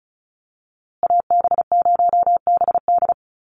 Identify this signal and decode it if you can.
Morse code, heard as a keyed tone, about 34 wpm, 700 Hz. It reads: AB0BD
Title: USA Call Signs Repeated 2x 35wpm